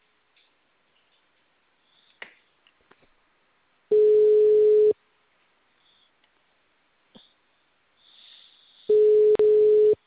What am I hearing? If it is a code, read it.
Ringback tone (Europe, 425 Hz).